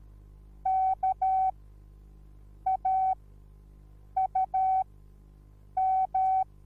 Morse code, heard as a keyed tone, about 13 wpm, spaced as for 6 wpm, 760 Hz. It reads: KAUM